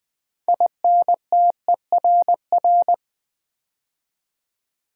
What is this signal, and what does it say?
Morse code, heard as a keyed tone, about 20 wpm, 700 Hz. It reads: INTERR
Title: Build A Word - 3 Letters At A Time - 9-Letter Words - 20wpm